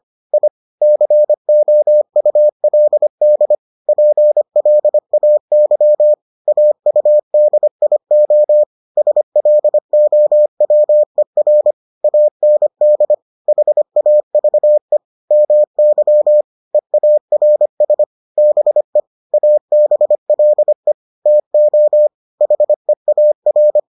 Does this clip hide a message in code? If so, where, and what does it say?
Yes: Morse code from 0.3 to 23.8 s, heard as a keyed tone, about 25 wpm, 600 Hz. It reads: I COULD PLAY AUDIO SLOWER AND HAVE MY EARS BE ABLE TO HEAR